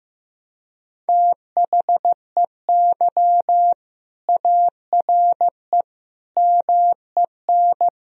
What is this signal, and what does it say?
Morse code, heard as a keyed tone, about 15 wpm, 700 Hz. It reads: THEY ARE MEN
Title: Sentences from Top 1000 Words - Review Part 2 - 15wpm